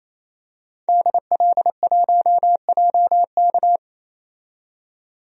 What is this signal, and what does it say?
Morse code, heard as a keyed tone, about 28 wpm, 700 Hz. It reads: DL1JK
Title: DXCC-100 Call Signs - 5 Characters - 28wpm